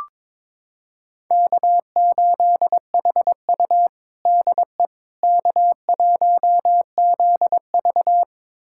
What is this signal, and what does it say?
Morse code, heard as a keyed tone, about 21 wpm, 700 Hz. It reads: K8HU DE K1ZV
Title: USA Callsign DE Callsign - 4 Characters - 22wpm